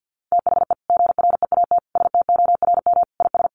Morse code, heard as a keyed tone, about 50 wpm, 700 Hz. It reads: THE GREAT STORM IS
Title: Sentences from Top 1500 Words 50wpm